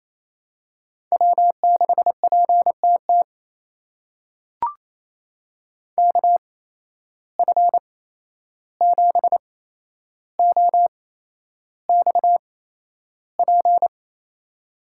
Morse code, heard as a keyed tone, about 28 wpm, spaced as for 11 wpm, 700 Hz. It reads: W6PTT KF7OXP
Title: USA Call Signs - 8x Character Spacing - 28wpm